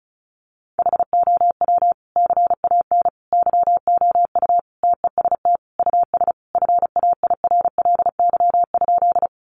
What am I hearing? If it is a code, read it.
Morse code, heard as a keyed tone, about 34 wpm, 700 Hz. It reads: HOW CAN YOU TEST US FAIRLY?
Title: Sentences from Top 1100 Words 35wpm